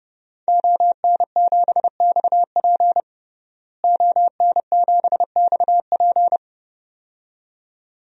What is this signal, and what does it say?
Morse code, heard as a keyed tone, about 30 wpm, 700 Hz. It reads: ON7XP ON7XP
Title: DXCC-100 Call Signs - 5 Characters - Repeated 2x - 30wpm